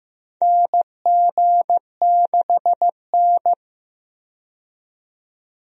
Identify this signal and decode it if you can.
Morse code, heard as a keyed tone, about 15 wpm, 700 Hz. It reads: NG6N
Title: USA Call Signs - 4 Characters - 15wpm